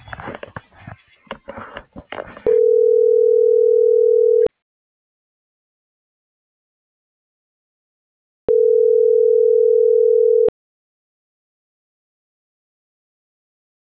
Ringback tone (North America), 440 + 480 Hz.